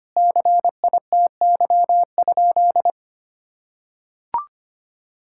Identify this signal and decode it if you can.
Morse code, heard as a keyed tone, about 25 wpm, 700 Hz. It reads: CITY?
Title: Sentences from Top 1000 Words - Review Part 5 - 25wpm